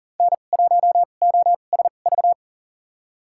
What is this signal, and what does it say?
Morse code, heard as a keyed tone, about 40 wpm, spaced as for 29 wpm, 700 Hz. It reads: N1OSV
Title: USA Call Signs - 5 Characters - 2x Character Spacing - 40wpm